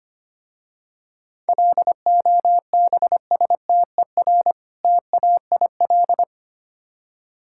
Morse code, heard as a keyed tone, about 25 wpm, 700 Hz. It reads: LOBSTER TAIL
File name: practice / 25wpm / Easy Words with Hints - 25wpm